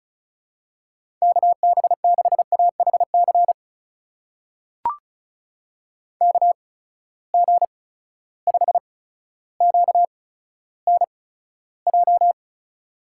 Morse code, heard as a keyed tone, about 36 wpm, spaced as for 15 wpm, 700 Hz. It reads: KB6AHC KG5QNJ
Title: USA Call Signs - 6 Characters - 8x Character Spacing - 35wpm